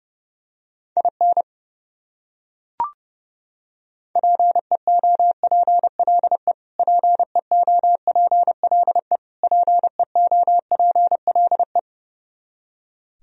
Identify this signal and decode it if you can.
Morse code, heard as a keyed tone, about 31 wpm, 700 Hz. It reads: IN PEOPLE PEOPLE PEOPLE
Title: Top 100 Words Repeated 30wpm